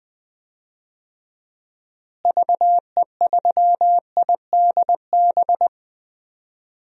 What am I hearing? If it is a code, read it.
Morse code, heard as a keyed tone, about 20 wpm, 700 Hz. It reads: VE3IDB
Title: DXCC-100 Call Signs - 6 Characters - Repeated 2x - 20wpm